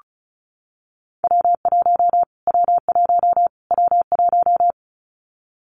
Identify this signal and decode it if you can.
Morse code, heard as a keyed tone, about 34 wpm, 700 Hz. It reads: W1 W1 W1
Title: Callsign Prefixes for Top 25 DXCC - Repeated 3x - 35wpm